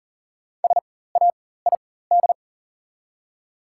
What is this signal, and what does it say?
Morse code, heard as a keyed tone, about 40 wpm, spaced as for 19 wpm, 700 Hz. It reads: SAID
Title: 4x Character Spacing - Top 300 Words - 40wpm